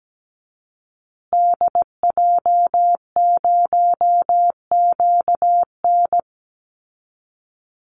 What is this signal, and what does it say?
Morse code, heard as a keyed tone, about 17 wpm, 700 Hz. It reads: DJ0QN
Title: Call Signs Repeated 2x - 17wpm